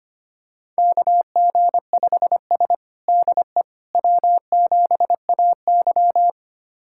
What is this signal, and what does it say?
Morse code, heard as a keyed tone, about 25 wpm, 700 Hz. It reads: KG5S DE W7AY
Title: USA Callsign DE Callsign - 4 Characters - 25wpm